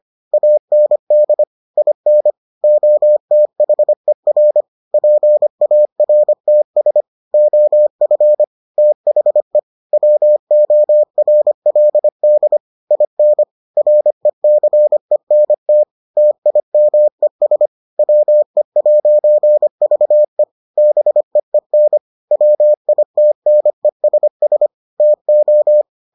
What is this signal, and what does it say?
Morse code, heard as a keyed tone, about 25 wpm, 600 Hz. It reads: AND IN OTHER PARTS OF THE WORLD IN RECENT TIMES WE'VE BEEN WITNESS TO